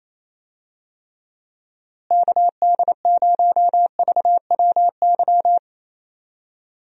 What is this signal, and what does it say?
Morse code, heard as a keyed tone, about 27 wpm, 700 Hz. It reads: KD0VWY
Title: USA Call Signs - 6 Characters - Repeated 2x - 28wpm